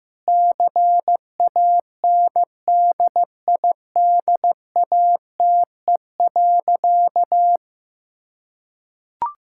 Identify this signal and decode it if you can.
Morse code, heard as a keyed tone, about 15 wpm, 700 Hz. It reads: CANDIDATE.
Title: Sentences from Top 1600 Words 15wpm